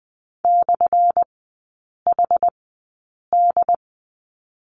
Morse code, heard as a keyed tone, about 20 wpm, spaced as for 8 wpm, 700 Hz. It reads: /HD
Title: 178 Lesson 29e - Slash - Three Characters 20wpm